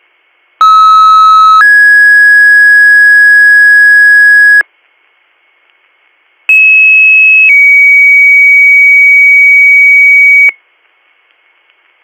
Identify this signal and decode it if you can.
Two-tone page, heard as a one-second tone, then a three-second tone, about 1290 then 1650 Hz, about 2600 then 2250 Hz. It is heard from 0.6 to 10.5 s.